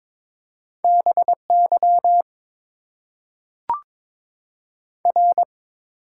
Morse code, heard as a keyed tone, about 22 wpm, 700 Hz. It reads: BY R
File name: Top 90 QSO Elements - 6x Character Spacing - 22wpm